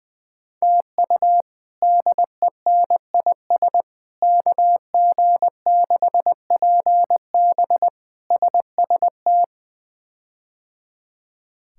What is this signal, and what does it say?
Morse code, heard as a keyed tone, about 21 wpm, 700 Hz. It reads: TU DENIS KG6PB SST
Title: K1USN SST QSO Practice - 20wpm